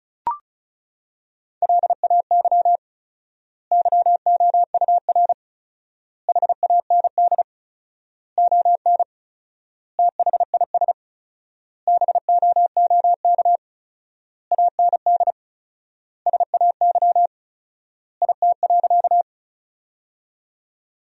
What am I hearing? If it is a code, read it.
Morse code, heard as a keyed tone, about 34 wpm, 700 Hz. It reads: LAY YOUR HAND ON THIS BOOK AND SAY IT.
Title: 4x Word-Spacing - Sentences from Top 400 Words - 35wpm